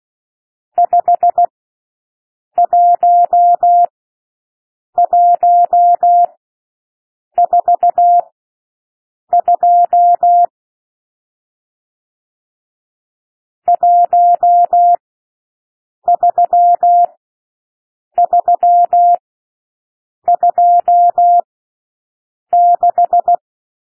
Morse code, heard as a keyed tone, about 16 wpm, spaced as for 6 wpm, 700 Hz. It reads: 51142 13326
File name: SOUND NUMBERS 16 wpm